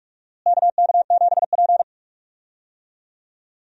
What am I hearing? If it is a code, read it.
Morse code, heard as a keyed tone, about 41 wpm, 700 Hz. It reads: KK7P
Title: USA Call Signs - 4 Characters - Repeated 2x - 45wpm